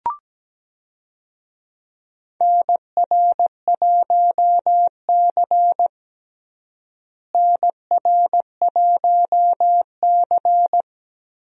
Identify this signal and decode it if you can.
Morse code, heard as a keyed tone, about 17 wpm, 700 Hz. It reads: NR1C NR1C
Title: USA Call Signs - 4 Characters - Repeated 3x - 17wpm